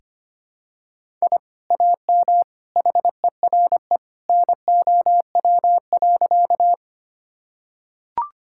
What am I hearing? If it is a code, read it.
Morse code, heard as a keyed tone, about 25 wpm, 700 Hz. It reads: I AM HERE NOW.
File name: practice / 25wpm / Sentences with 2-Word Phrases - 25wpm